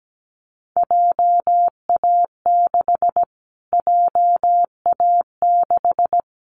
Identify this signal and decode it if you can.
Morse code, heard as a keyed tone, about 17 wpm, 700 Hz. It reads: JA6 JA6
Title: Callsign Prefixes for Top 25 DXCC - Repeated 3x - 17wpm